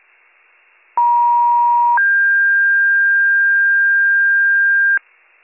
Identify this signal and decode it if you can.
Two-tone page, heard as a one-second tone, then a three-second tone, about 950 then 1600 Hz, with noise. It is heard from 1.0 to 5.0 s.